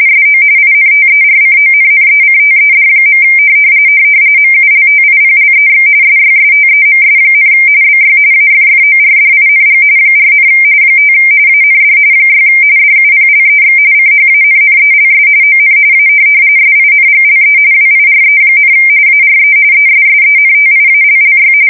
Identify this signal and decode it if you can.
RTTY, about 45.45 baud. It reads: SO INTERCONNECTED. MULTIPLEX SYSTEMS, BY AFFORDING 4, 5 AND 6 DUPLEX CHANNELS ON A SINGLE WIRE, ARE ESSENTIALLY
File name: fsk170 / morkrum